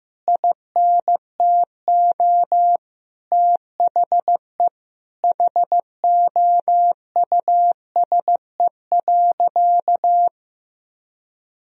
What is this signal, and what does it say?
Morse code, heard as a keyed tone, about 15 wpm, 700 Hz. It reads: INTO THE HOUSE.
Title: Sentences from Top 1000 Words - Review Part 1 - 15wpm